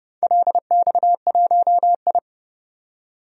Morse code, heard as a keyed tone, about 32 wpm, 700 Hz. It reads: LX1I
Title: DXCC-100 Call Signs - 4 Characters - 30wpm